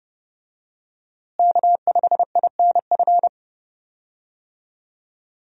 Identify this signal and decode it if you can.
Morse code, heard as a keyed tone, about 30 wpm, 700 Hz. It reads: K5INF
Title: USA Call Signs - 5 Characters - Repeated 3x - 30wpm